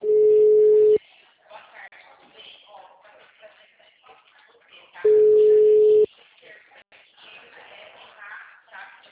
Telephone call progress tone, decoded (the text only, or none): ringback tone